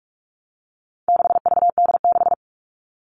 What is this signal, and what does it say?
Morse code, heard as a keyed tone, about 44 wpm, 700 Hz. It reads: 6VD6
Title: All Cut Numbers - 4-Digit 45wpm